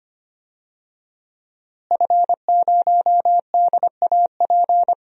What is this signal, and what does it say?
Morse code, heard as a keyed tone, about 25 wpm, 700 Hz. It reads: F0DAP
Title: DXCC-100 Call Signs - 5 Characters - Repeated 2x - 25wpm